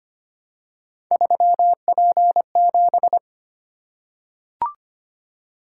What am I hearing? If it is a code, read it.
Morse code, heard as a keyed tone, about 25 wpm, 700 Hz. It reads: 3P7
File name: Three Letter-Number Combinations - 25wpm